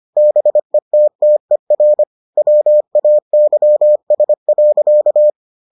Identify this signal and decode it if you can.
Morse code, heard as a keyed tone, about 25 wpm, 600 Hz. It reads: BETTER WAYS.